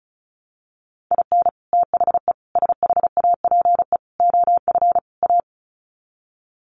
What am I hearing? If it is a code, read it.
Morse code, heard as a keyed tone, about 34 wpm, 700 Hz. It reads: IN THE SHAPE OF A
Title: Sets of 5 Words Encore 35wpm